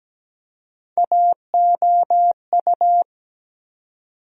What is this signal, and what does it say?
Morse code, heard as a keyed tone, about 17 wpm, 700 Hz. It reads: AOU
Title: USA Call Sign Postfixes - 3 Letters 17wpm